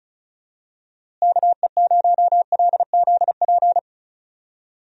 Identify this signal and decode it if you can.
Morse code, heard as a keyed tone, about 36 wpm, 700 Hz. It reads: KE0LZP
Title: USA Call Signs - 6 Characters - Repeated 3x - 35wpm